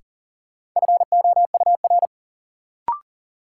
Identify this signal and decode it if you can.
Morse code, heard as a keyed tone, about 40 wpm, 700 Hz. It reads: FOUR
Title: Sentences from Top 300 Words - New Word Drill - 40wpm